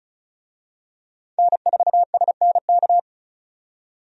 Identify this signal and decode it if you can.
Morse code, heard as a keyed tone, about 36 wpm, 700 Hz. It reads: N4SNK
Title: USA Call Signs - 5 Characters - 35wpm